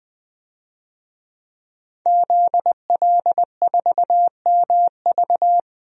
Morse code, heard as a keyed tone, about 21 wpm, 700 Hz. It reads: ZL4MV
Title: DXCC-100 Call Signs - 5 Characters - 20wpm